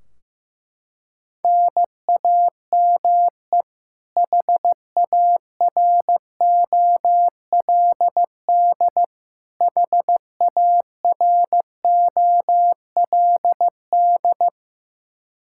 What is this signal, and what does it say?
Morse code, heard as a keyed tone, about 15 wpm, 700 Hz. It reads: NAME HAROLD HAROLD
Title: Practice QSO - Exchange 1 - Beginning - 15wpm